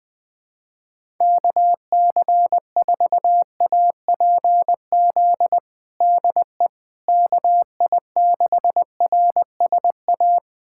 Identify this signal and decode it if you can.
Morse code, heard as a keyed tone, about 21 wpm, 700 Hz. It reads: KC4APZ DE KI6RSA
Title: USA Callsign DE Callsign - 6 Characters - 20wpm